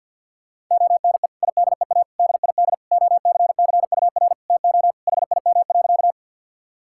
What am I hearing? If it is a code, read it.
Morse code, heard as a keyed tone, about 50 wpm, 700 Hz. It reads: ONE IDEA DID OCCUR TO HIM.
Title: Sentences from Top 1000 Words - Review Part 6 - 50wpm